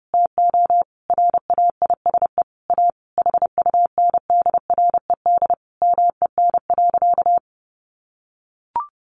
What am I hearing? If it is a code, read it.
Morse code, heard as a keyed tone, about 30 wpm, 700 Hz. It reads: TO RAISE A HUNDRED MEN.
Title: Sentences from Top 1400 Words 30wpm